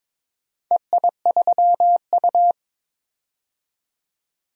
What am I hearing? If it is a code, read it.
Morse code, heard as a keyed tone, about 21 wpm, 700 Hz. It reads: EI3U
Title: DXCC-100 Call Signs - 4 Characters - 22wpm